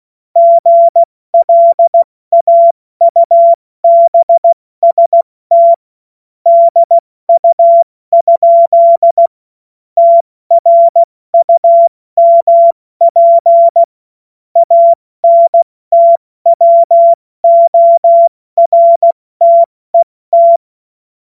Morse code, heard as a keyed tone, about 16 wpm, 680 Hz. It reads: GLAUBST DU? TRUMP ANTWORTET
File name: Morsekurs / Witze / Witz 09. Februar 2021, 70 F80